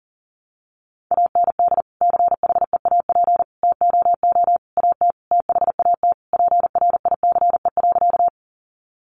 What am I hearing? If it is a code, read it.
Morse code, heard as a keyed tone, about 40 wpm, 700 Hz. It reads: AND CHEAP TOO AT THAT PRICE.
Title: Sentences from Top 1200 Words 40wpm